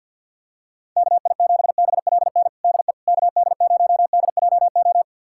Morse code, heard as a keyed tone, about 49 wpm, 700 Hz. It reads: KI7BLN DE KD0DJO
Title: USA Callsign DE Callsign 50wpm